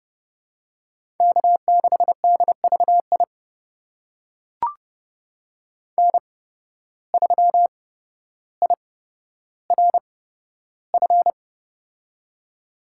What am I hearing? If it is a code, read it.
Morse code, heard as a keyed tone, about 31 wpm, spaced as for 13 wpm, 700 Hz. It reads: K6DVI N3IRF
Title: USA Call Signs - 5 Characters - 8x Character Spacing - 30wpm